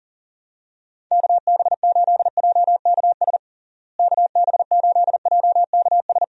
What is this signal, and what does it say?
Morse code, heard as a keyed tone, about 40 wpm, 700 Hz. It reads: KB8JKS KB8JKS
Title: USA Call Signs - 6 Characters - Repeated 2x - 40wpm